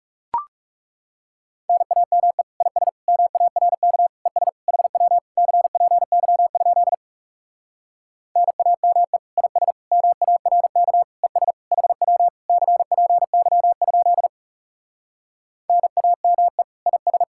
Morse code, heard as a keyed tone, about 40 wpm, 700 Hz. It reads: NAME IS MARK ES HW CPY? NAME IS MARK ES HW CPY? NAME IS